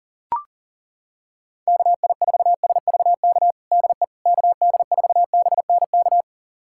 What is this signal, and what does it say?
Morse code, heard as a keyed tone, about 40 wpm, 700 Hz. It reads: KI4SVK DE KD4BNK
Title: USA Callsign DE Callsign - 6 Characters - 40wpm